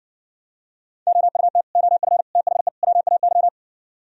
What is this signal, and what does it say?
Morse code, heard as a keyed tone, about 58 wpm, 700 Hz. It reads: OUT OF THE WAY